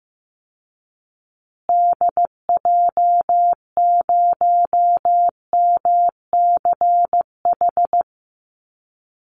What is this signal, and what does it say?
Morse code, heard as a keyed tone, about 14 wpm, 700 Hz. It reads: DJ0MCH
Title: Call Signs Repeated 2x - 15wpm